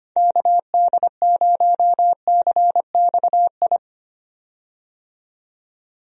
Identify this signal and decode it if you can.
Morse code, heard as a keyed tone, about 25 wpm, 700 Hz. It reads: KD0CXI
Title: USA Call Signs - 6 Characters - 25wpm